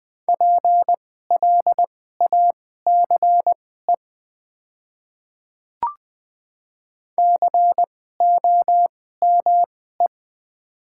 Morse code, heard as a keyed tone, about 20 wpm, spaced as for 14 wpm, 700 Hz. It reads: PLACE COME
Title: 2x Character Spacing - Top 200 Words - 20wpm